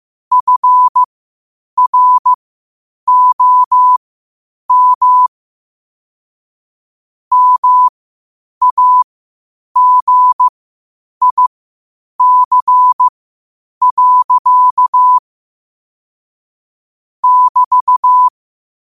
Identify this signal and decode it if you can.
Morse code, heard as a keyed tone, about 15 wpm, spaced as for 8 wpm, 1000 Hz. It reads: FROM MAGIC. <BT>
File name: qotdfiles / 8wpm